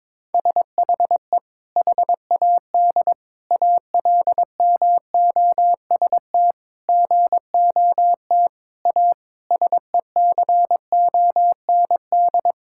Morse code, heard as a keyed tone, about 22 wpm, 700 Hz. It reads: SHE HAD ALMOST GOT A SECOND
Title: Sentences from Top 1000 Words - Review Part 5 - 22wpm